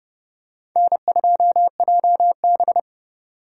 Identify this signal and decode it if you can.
Morse code, heard as a keyed tone, about 30 wpm, 700 Hz. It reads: N2JB
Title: USA Call Signs - 4 Characters - 4x Character Spacing - 30wpm